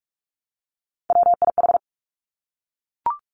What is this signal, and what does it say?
Morse code, heard as a keyed tone, about 46 wpm, 700 Hz. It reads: WIH
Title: Three Challenging Letter-Number Combinations 45wpm